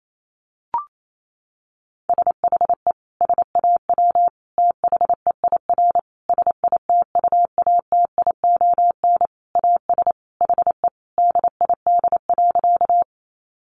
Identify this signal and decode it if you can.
Morse code, heard as a keyed tone, about 28 wpm, 700 Hz. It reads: SHE SAW THEIR SITUATION AS HE DID.
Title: Sentences from Top 1200 Words 28wpm